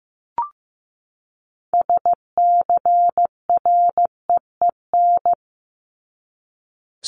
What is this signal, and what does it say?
Morse code, heard as a keyed tone, about 15 wpm, 700 Hz. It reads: SCREEN
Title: Top 751-1000 Words 15wpm